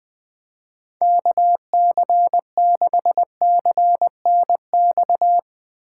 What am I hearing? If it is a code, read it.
Morse code, heard as a keyed tone, about 20 wpm, 700 Hz. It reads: KC6CNX
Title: USA Call Signs - 6 Characters - 3x Character Spacing - 20wpm